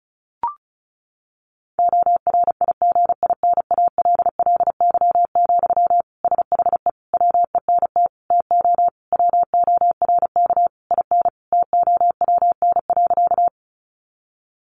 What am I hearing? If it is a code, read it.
Morse code, heard as a keyed tone, about 34 wpm, 700 Hz. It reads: ORIGINALLY, SHE WENT TO WORK IN TOWN.
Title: Sentences from Top 1500 Words 35wpm